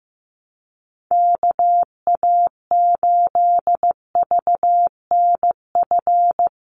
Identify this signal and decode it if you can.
Morse code, heard as a keyed tone, about 15 wpm, 700 Hz. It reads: KA8VNF
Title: USA Call Signs 15wpm